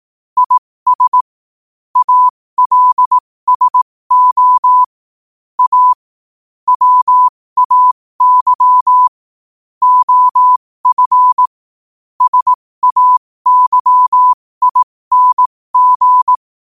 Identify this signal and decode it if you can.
Morse code, heard as a keyed tone, about 18 wpm, spaced as for 15 wpm, 1000 Hz. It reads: IS ALSO A WAY OF SAYING
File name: qotdfiles / 15wpm